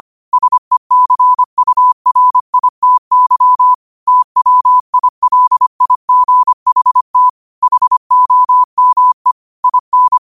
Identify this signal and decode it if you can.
Morse code, heard as a keyed tone, about 25 wpm, 1000 Hz. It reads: SECURITY TWILIGHT HOME IN
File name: qotdfiles / 25wpm